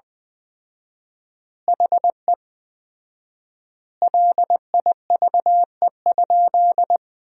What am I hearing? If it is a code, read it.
Morse code, heard as a keyed tone, about 21 wpm, 700 Hz. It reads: HE LIVE?
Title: 4x Word-Spacing - Sentences from Top 200 Words - 20wpm